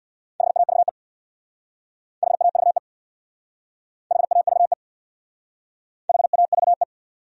Morse code, heard as a keyed tone, about 68 wpm, 700 Hz. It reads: HAVE HAVE HAVE HAVE